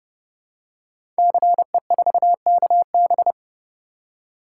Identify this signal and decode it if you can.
Morse code, heard as a keyed tone, about 30 wpm, 700 Hz. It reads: CE4KB
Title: Code Groups - 5 Letter:Numbers - 30wpm